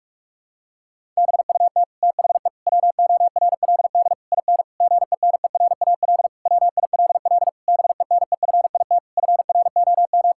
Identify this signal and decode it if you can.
Morse code, heard as a keyed tone, about 41 wpm, 700 Hz. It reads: BUT THE WORLD IN GENERAL WILL BENEFIT FROM
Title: Sentences from Top 1000 Words - Review Part 6 - 45wpm